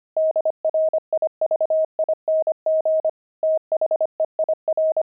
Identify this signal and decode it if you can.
Morse code, heard as a keyed tone, about 25 wpm, 630 Hz. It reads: DRIVING THEIR
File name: pangram-fs-25-cs-25-jaded